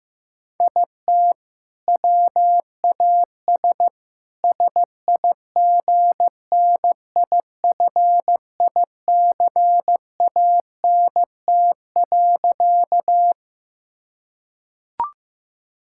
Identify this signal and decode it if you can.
Morse code, heard as a keyed tone, about 15 wpm, 700 Hz. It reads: IT WAS SIGNIFICANT.
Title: Sentences from Top 1000 Words - Review Part 6 - 15wpm